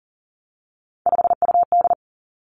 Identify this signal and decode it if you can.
Morse code, heard as a keyed tone, about 40 wpm, 700 Hz. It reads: 5UD
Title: Three Challenging Letter-Number Combinations 40wpm